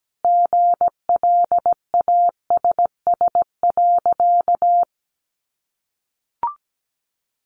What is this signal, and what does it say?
Morse code, heard as a keyed tone, about 17 wpm, 700 Hz. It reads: GLASS.
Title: Sentences from Top 1300 Words 17wpm